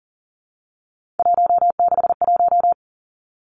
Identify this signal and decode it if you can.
Morse code, heard as a keyed tone, about 38 wpm, 700 Hz. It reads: 161
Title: Three Challenging Letter-Number Combinations 40wpm